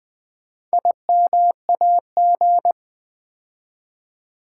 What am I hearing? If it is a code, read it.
Morse code, heard as a keyed tone, about 20 wpm, 700 Hz. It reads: IMAG
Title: Build A Word - 4 Letters At A Time - 11-Letter Words - 20wpm